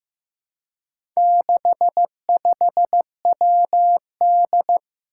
Morse code, heard as a keyed tone, about 15 wpm, 700 Hz. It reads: 65WD
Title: Four Challenging Letter-Number Combinations 15wpm